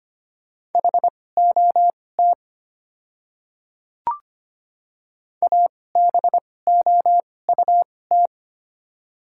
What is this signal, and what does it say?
Morse code, heard as a keyed tone, about 25 wpm, spaced as for 18 wpm, 700 Hz. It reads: HOT ABOUT
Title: 2x Character Spacing - Top 300 Words - 25wpm